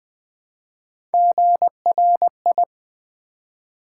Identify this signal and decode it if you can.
Morse code, heard as a keyed tone, about 20 wpm, 700 Hz. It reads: GRI
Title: Build A Word - 1 Letter At A Time - 5-Letter Words - 20wpm